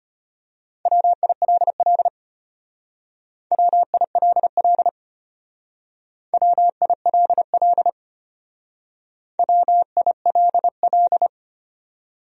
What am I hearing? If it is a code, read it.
Morse code, heard as a keyed tone, about 31 wpm, 700 Hz. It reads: WILL WILL WILL WILL
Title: Speed-Racing Top 500 Words 25wpm